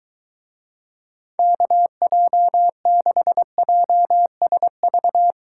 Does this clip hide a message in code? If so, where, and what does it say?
Yes: Morse code from 1.4 to 5.3 s, heard as a keyed tone, about 23 wpm, 700 Hz. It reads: KJ6JSV